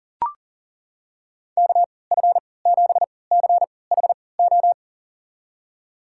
Morse code, heard as a keyed tone, about 40 wpm, spaced as for 23 wpm, 700 Hz. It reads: KF7CHO
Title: USA Call Signs - 6 Characters - 3x Character Spacing - 40wpm